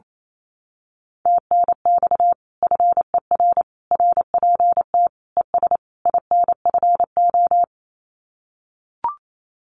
Morse code, heard as a keyed tone, about 28 wpm, 700 Hz. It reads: TNX FER RPT ES INFO
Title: Common QSO Phrases Encore 28wpm